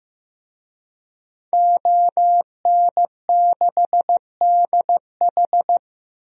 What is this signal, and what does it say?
Morse code, heard as a keyed tone, about 15 wpm, 700 Hz. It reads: ON6DH
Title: DXCC-100 Call Signs - 5 Characters - Repeated 3x - 15wpm